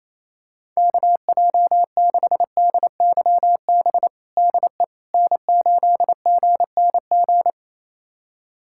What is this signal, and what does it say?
Morse code, heard as a keyed tone, about 27 wpm, 700 Hz. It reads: KJ6DYB DE N8GNG
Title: USA Callsign DE Callsign 28wpm